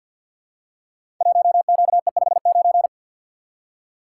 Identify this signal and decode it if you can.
Morse code, heard as a keyed tone, about 52 wpm, 700 Hz. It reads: 1QE59
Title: Code Groups - 5 Letter:Numbers - 50wpm